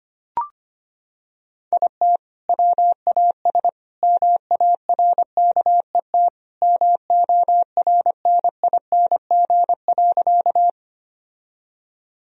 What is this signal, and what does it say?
Morse code, heard as a keyed tone, about 25 wpm, 700 Hz. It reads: IT WAS MARKET MORNING.
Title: Sentences from Top 1000 Words - Review Part 3 - 25wpm